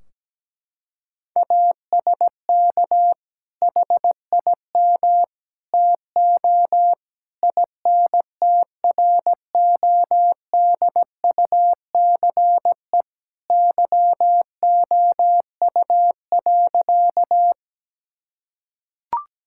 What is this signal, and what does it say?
Morse code, heard as a keyed tone, about 17 wpm, 700 Hz. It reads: ASK HIM TO INTRODUCE YOU.
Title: Sentences from Top 1600 Words 17wpm